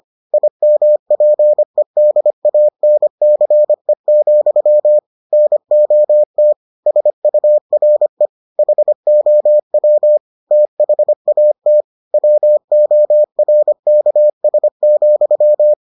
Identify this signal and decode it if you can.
Morse code, heard as a keyed tone, about 25 wpm, 600 Hz. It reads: IMPEDANCE, NOT SURE HOW THAT WORKS,